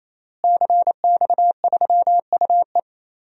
Morse code, heard as a keyed tone, about 27 wpm, 700 Hz. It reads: CX3UE
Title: DXCC-100 Call Signs - 5 Characters - Repeated 3x - 28wpm